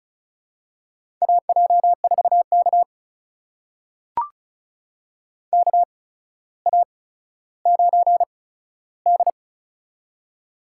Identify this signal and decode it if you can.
Morse code, heard as a keyed tone, about 36 wpm, spaced as for 15 wpm, 700 Hz. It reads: AJ4K KA9D
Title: USA Call Signs - 4 Characters - 8x Character Spacing - 35wpm